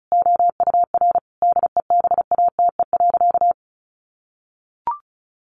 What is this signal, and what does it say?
Morse code, heard as a keyed tone, about 34 wpm, 700 Hz. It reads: OUR DEBATE.
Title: Sentences from Top 1100 Words 35wpm